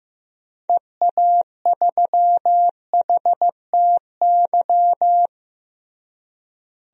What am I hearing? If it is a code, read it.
Morse code, heard as a keyed tone, about 15 wpm, 700 Hz. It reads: EA3HTY
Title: DXCC-100 Call Signs - 6 Characters - Repeated 2x - 15wpm